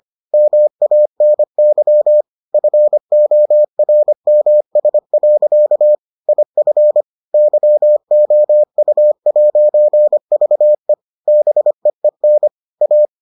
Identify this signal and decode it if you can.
Morse code, heard as a keyed tone, about 25 wpm, 600 Hz. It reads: MANY FORMS. IF YOU'VE BEEN A